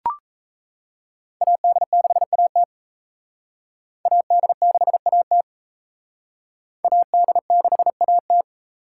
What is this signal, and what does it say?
Morse code, heard as a keyed tone, about 34 wpm, 700 Hz. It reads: AD6AT AD6AT AD6AT
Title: Speed Racing - USA 2x2 Call Signs 28wpm